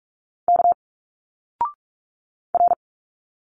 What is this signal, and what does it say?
Morse code, heard as a keyed tone, about 45 wpm, 700 Hz. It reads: K R
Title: Single Letters 45wpm